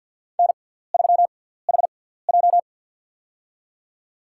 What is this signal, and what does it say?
Morse code, heard as a keyed tone, about 52 wpm, spaced as for 17 wpm, 700 Hz. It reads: N3HJ
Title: USA Call Signs - 4 Characters - 6x Character Spacing - 50wpm